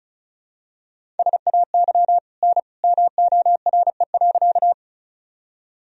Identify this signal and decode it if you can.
Morse code, heard as a keyed tone, about 34 wpm, 700 Hz. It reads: SAY N MORE.
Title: Sentences from Top 200 Words and Abbreviations 35wpm